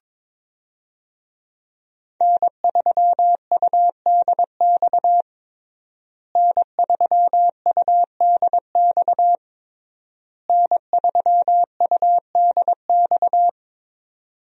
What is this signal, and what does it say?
Morse code, heard as a keyed tone, about 22 wpm, 700 Hz. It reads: N3UDX N3UDX N3UDX
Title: USA Call Signs - 5 Characters - Repeated 3x - 22wpm